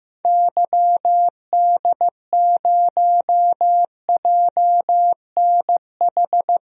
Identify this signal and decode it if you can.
Morse code, heard as a keyed tone, about 15 wpm, 700 Hz. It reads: YD0JNH
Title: DXCC-100 Call Signs - 6 Characters - 15wpm